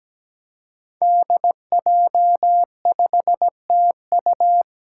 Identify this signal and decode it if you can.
Morse code, heard as a keyed tone, about 17 wpm, 700 Hz. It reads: DJ5TU
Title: Five Challenging Letter-Number Combinations 17wpm